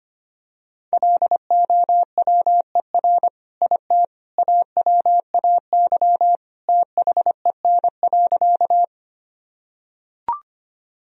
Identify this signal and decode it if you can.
Morse code, heard as a keyed tone, about 25 wpm, 700 Hz. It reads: LOWER IT AWAY THEN.
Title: Sentences from Top 1000 Words - Review Part 4 - 25wpm